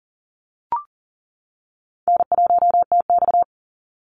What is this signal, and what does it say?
Morse code, heard as a keyed tone, about 40 wpm, 700 Hz. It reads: N1TX
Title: Easy Call Signs - 40wpm